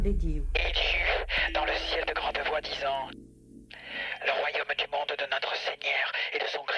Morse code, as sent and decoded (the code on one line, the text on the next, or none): none